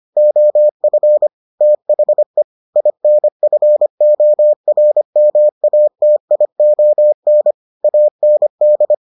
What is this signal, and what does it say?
Morse code, heard as a keyed tone, about 25 wpm, 600 Hz. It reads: OF THE INFORMATION AND